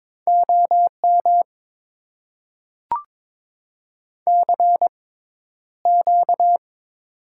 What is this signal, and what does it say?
Morse code, heard as a keyed tone, about 22 wpm, spaced as for 11 wpm, 700 Hz. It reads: OM CQ
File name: Top 90 QSO Elements - 6x Character Spacing - 22wpm